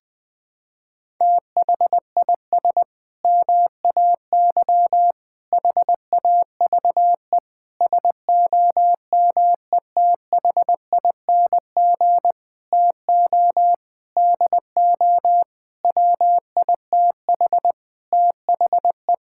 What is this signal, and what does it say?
Morse code, heard as a keyed tone, about 20 wpm, 700 Hz. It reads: THIS MAY HAVE SOMETHING TO DO WITH THE
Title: Sentences with 8-Word Phrases - 20wpm